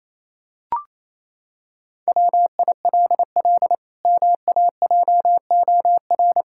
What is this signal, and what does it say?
Morse code, heard as a keyed tone, about 28 wpm, 700 Hz. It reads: WILL MAJOR
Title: Pangram Challenge v2 - 28wpm